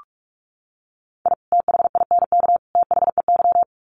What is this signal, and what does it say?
Morse code, heard as a keyed tone, about 44 wpm, 700 Hz. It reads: I THINK THEY
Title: Sets of 3 Words 45wpm